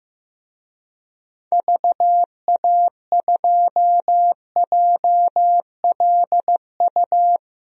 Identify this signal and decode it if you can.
Morse code, heard as a keyed tone, about 15 wpm, 700 Hz. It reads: VA2JLU
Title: DXCC-100 Call Signs - 6 Characters - Repeated 3x - 15wpm